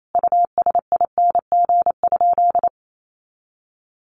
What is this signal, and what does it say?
Morse code, heard as a keyed tone, about 28 wpm, 700 Hz. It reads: USING?
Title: Common QSO Phrases 28wpm